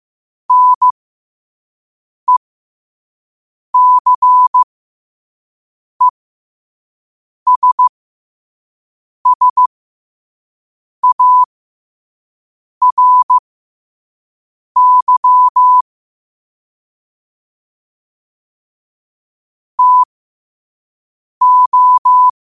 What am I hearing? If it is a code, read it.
Morse code, heard as a keyed tone, about 15 wpm, spaced as for 5 wpm, 1000 Hz. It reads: NECESSARY TO
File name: qotdfiles / 5wpm